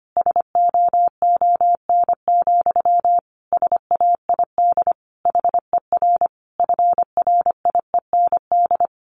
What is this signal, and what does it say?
Morse code, heard as a keyed tone, about 25 wpm, 700 Hz. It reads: SOON, SAID HER FRIEND
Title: Sentences from Top 1500 Words 25wpm